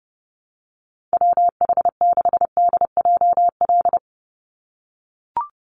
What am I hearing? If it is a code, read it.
Morse code, heard as a keyed tone, about 30 wpm, 700 Hz. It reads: WH6DJL
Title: USA Call Signs 30wpm